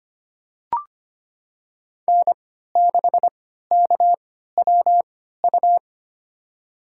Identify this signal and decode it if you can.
Morse code, heard as a keyed tone, about 25 wpm, spaced as for 14 wpm, 700 Hz. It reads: N6KWU
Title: USA Call Signs - 5 Characters - 3x Character Spacing - 25wpm